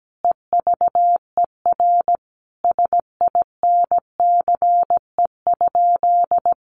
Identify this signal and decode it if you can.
Morse code, heard as a keyed tone, about 17 wpm, 700 Hz. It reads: EVER SINCE?
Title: Sentences from Top 700 Words 17wpm